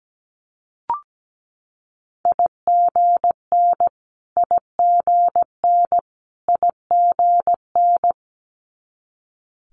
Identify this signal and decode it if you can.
Morse code, heard as a keyed tone, about 17 wpm, 700 Hz. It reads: IGN IGN IGN
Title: Multi-letter Phonemes Repeated 3x 17wpm